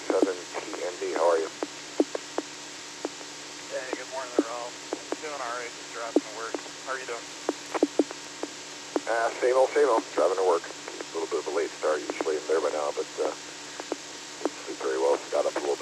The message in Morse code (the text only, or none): none